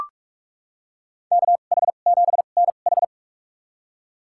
Morse code, heard as a keyed tone, about 43 wpm, spaced as for 32 wpm, 700 Hz. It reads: KH7NH
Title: USA Call Signs - 5 Characters - 2x Character Spacing - 45wpm